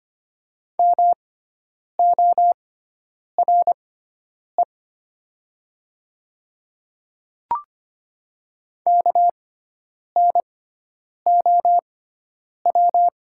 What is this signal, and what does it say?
Morse code, heard as a keyed tone, about 25 wpm, spaced as for 9 wpm, 700 Hz. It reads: MORE KNOW
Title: 6x Character Spacing - Top 200 Words - 25wpm